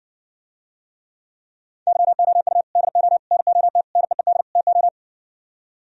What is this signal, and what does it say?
Morse code, heard as a keyed tone, about 62 wpm, 700 Hz. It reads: YOU DO NOT NEED TO